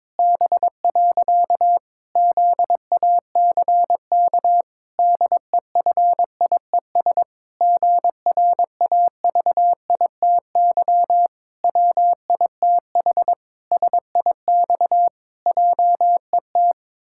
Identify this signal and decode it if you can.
Morse code, heard as a keyed tone, about 22 wpm, 700 Hz. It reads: B. ZACK DEFIES GRAVITY WITH SIX JET
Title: Pangram Challenge v2 - 22wpm